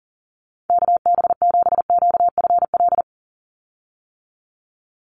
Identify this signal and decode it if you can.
Morse code, heard as a keyed tone, about 40 wpm, 700 Hz. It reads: KB7QFL
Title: USA Call Signs 40wpm